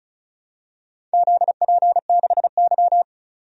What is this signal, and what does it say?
Morse code, heard as a keyed tone, about 36 wpm, 700 Hz. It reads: ZP6Y
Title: DXCC-100 Call Signs - 4 Characters - 35wpm